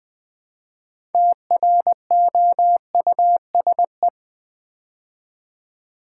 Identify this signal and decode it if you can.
Morse code, heard as a keyed tone, about 20 wpm, 700 Hz. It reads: TROUSE
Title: Build A Word - 2 Letters At A Time - 8-Letter Words - 20wpm